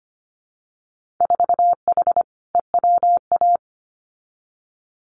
Morse code, heard as a keyed tone, about 25 wpm, 700 Hz. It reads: 4H EWA
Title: Winter Field Day Exchanges Encore 25wpm